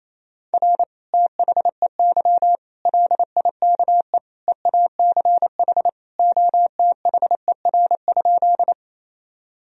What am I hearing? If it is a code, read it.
Morse code, heard as a keyed tone, about 28 wpm, 700 Hz. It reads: R THEY LIKE EACH OTHER?
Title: Sentences from Top 200 Words and Abbreviations 28wpm